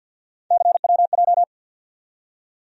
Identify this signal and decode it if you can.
Morse code, heard as a keyed tone, about 48 wpm, 700 Hz. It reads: CWJ
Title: USA Call Sign Postfixes - 3 Letters 50wpm